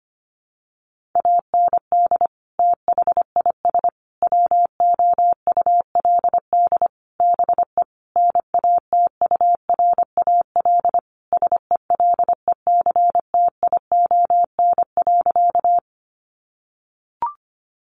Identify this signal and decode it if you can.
Morse code, heard as a keyed tone, about 25 wpm, 700 Hz. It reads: AND THIS WOULD BE NATURAL SELECTION.
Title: Sentences from Top 1400 Words 25wpm